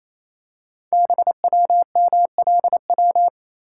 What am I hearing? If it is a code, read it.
Morse code, heard as a keyed tone, about 28 wpm, 700 Hz. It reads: BWMLW
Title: Code Groups - 5 Letters - 28wpm